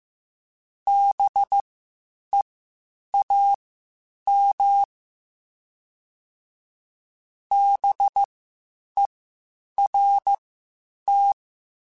Morse code, heard as a keyed tone, about 15 wpm, spaced as for 8 wpm, 800 Hz. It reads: BEAM BERT